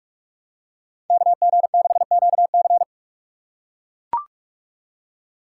Morse code, heard as a keyed tone, about 42 wpm, 700 Hz. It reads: KG6QC